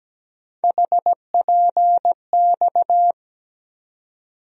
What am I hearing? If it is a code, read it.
Morse code, heard as a keyed tone, about 17 wpm, 700 Hz. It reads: HPX
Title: USA Call Sign Postfixes - 3 Letters 17wpm